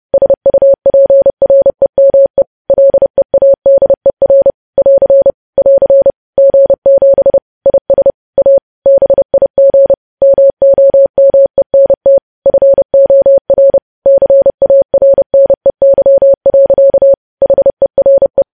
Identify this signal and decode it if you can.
Morse code, heard as a keyed tone, about 30 wpm, 550 Hz. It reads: SUPREME LEADER <AR> <AR> G7 IS A BIG MOMENT FOR CARNEY. HERE
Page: news headlines in morse code at 30 wpm